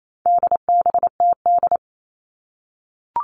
Morse code, heard as a keyed tone, about 28 wpm, 700 Hz. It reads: DBTD
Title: All Cut Numbers - 4-Digit 28wpm